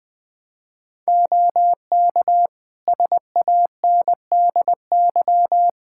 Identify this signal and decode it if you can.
Morse code, heard as a keyed tone, about 20 wpm, 700 Hz. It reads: OK SANDY
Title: K1USN SST QSO Practice 20wpm